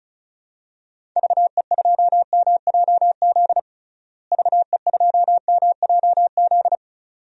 Morse code, heard as a keyed tone, about 36 wpm, 700 Hz. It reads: VE2MJZ VE2MJZ
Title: DXCC-100 Call Signs - 6 Characters - Repeated 2x - 35wpm